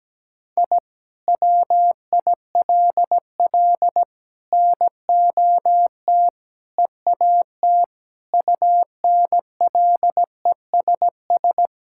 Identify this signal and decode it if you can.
Morse code, heard as a keyed tone, about 17 wpm, 700 Hz. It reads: I WILL NOT EAT UNLESS
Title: Sentences from Top 1000 Words - Review Part 6 - 17wpm